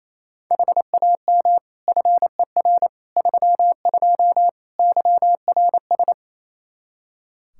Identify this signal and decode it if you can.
Morse code, heard as a keyed tone, about 27 wpm, 700 Hz. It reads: HAM FER 32 YRS
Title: Practice QSO - Exchange 3 - More Info - 28wpm